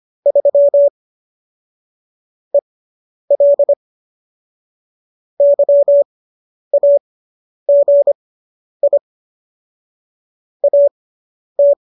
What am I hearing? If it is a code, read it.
Morse code, heard as a keyed tone, about 25 wpm, spaced as for 10 wpm, 570 Hz. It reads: 3 EL YAGI AT